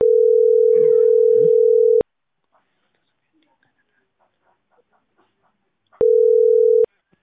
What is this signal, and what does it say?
Ringback tone (North America), 440 + 480 Hz.